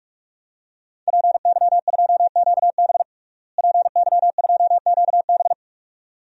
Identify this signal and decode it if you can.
Morse code, heard as a keyed tone, about 46 wpm, 700 Hz. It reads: PY2QB PY2QB
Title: DXCC-100 Call Signs - 5 Characters - Repeated 2x - 45wpm